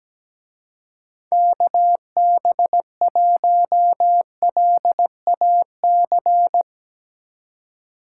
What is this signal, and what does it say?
Morse code, heard as a keyed tone, about 19 wpm, 700 Hz. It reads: KB1LAC
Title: Speed Racing - USA 2x3 Call Signs 17wpm